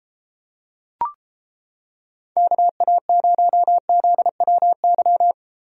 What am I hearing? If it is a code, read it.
Morse code, heard as a keyed tone, about 30 wpm, 700 Hz. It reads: KA0ZWY